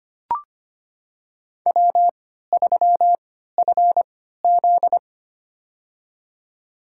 Morse code, heard as a keyed tone, about 25 wpm, spaced as for 14 wpm, 700 Hz. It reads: W3FZ
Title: USA Call Signs - 4 Characters - 3x Character Spacing - 25wpm